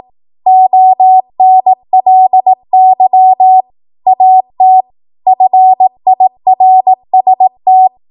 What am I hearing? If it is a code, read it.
Morse code, heard as a keyed tone, about 18 wpm, 750 Hz. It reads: ONLY AT FIRST
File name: Morse / 18WPM